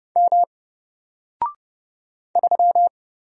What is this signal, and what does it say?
Morse code, heard as a keyed tone, about 29 wpm, 700 Hz. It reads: M 3
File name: Single-Letter-Number-30wpm